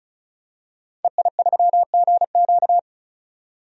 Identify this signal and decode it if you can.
Morse code, heard as a keyed tone, about 36 wpm, 700 Hz. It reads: EI3GQ
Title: DXCC-100 Call Signs - 5 Characters - 35wpm